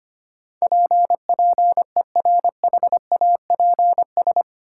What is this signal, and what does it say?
Morse code, heard as a keyed tone, about 25 wpm, 700 Hz. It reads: PPERHAPS
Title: Sentences from Top 1100 Words - New Word Drill - 25wpm